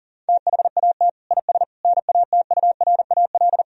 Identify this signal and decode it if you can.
Morse code, heard as a keyed tone, about 40 wpm, 700 Hz. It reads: THAT IS NATURAL
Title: Sentences from Top 1000 Words - Review Part 4 - 40wpm